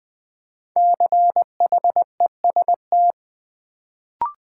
Morse code, heard as a keyed tone, about 20 wpm, 700 Hz. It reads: CHEST
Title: Build A Word - 1 Letter At A Time - 5-Letter Words - 20wpm